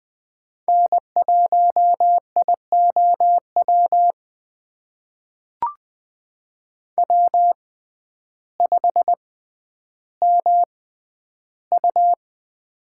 Morse code, heard as a keyed tone, about 20 wpm, spaced as for 11 wpm, 700 Hz. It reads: N1IOW W5MU
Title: USA Call Signs - 6x Character Spacing - 20wpm